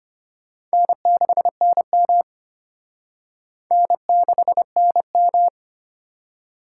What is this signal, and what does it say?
Morse code, heard as a keyed tone, about 33 wpm, 700 Hz. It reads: N6NM N6NM